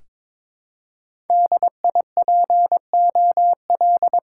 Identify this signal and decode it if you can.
Morse code, heard as a keyed tone, about 22 wpm, 700 Hz. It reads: D I P O L
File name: Top 90 QSO Elements - 6x Character Spacing - 22wpm